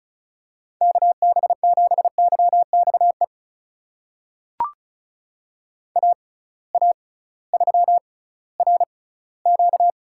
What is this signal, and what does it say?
Morse code, heard as a keyed tone, about 36 wpm, spaced as for 19 wpm, 700 Hz. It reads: KB7YXE AA3RQ
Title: USA Call Signs - 6x Character Spacing - 35wpm